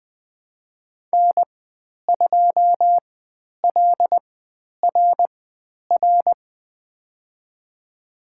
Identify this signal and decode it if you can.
Morse code, heard as a keyed tone, about 20 wpm, spaced as for 10 wpm, 700 Hz. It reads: N2LRR
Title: 102 Lesson 17g - Letter u - Call Signs 20wpm